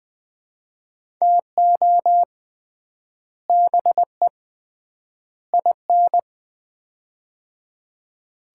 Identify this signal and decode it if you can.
Morse code, heard as a keyed tone, about 20 wpm, 700 Hz. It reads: TO BE IN
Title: 3x Word-Spacing - Sets of 3 Words - 20wpm